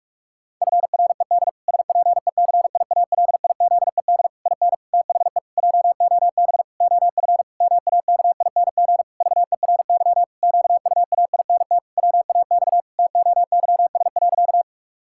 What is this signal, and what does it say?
Morse code, heard as a keyed tone, about 46 wpm, 700 Hz. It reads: FRED SPECIALIZED IN THE JOB OF MAKING VERY QUAINT WAX TOYS.
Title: Pangram Challenge v2 - 45wpm